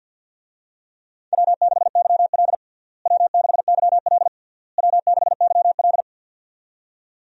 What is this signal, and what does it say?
Morse code, heard as a keyed tone, about 53 wpm, 700 Hz. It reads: W6YL W6YL W6YL
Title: USA Call Signs - 4 Characters - Repeated 3x - 50wpm